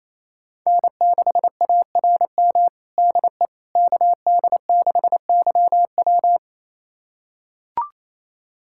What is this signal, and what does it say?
Morse code, heard as a keyed tone, about 28 wpm, 690 Hz. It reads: N6ARM DE KD6YW
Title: USA Callsign DE Callsign - 5 Characters - 28wpm